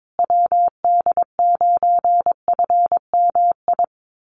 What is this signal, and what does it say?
Morse code, heard as a keyed tone, about 22 wpm, 700 Hz. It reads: WD9FMI
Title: USA Call Signs Repeated 3x 22wpm